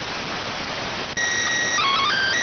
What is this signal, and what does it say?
SSTV (slow-scan television), PD 290.